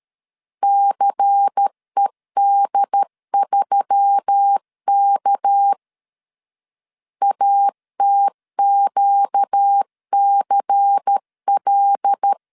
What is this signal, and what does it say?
Morse code, heard as a keyed tone, about 13 wpm, 800 Hz. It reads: CED3K ATQCL
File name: CW / DET / 1 6 1500C S C 12wpm 3533